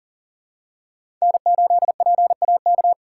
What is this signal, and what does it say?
Morse code, heard as a keyed tone, about 40 wpm, 700 Hz. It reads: N8PAK
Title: USA Call Signs - 5 Characters - 6x Character Spacing - 40wpm